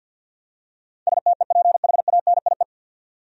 Morse code, heard as a keyed tone, about 51 wpm, 700 Hz. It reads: STEPHANIE
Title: USA Common Names - 50wpm